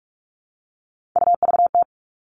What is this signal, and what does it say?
Morse code, heard as a keyed tone, about 44 wpm, 700 Hz. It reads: UVT
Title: All Cut Numbers - 3-Digit 45wpm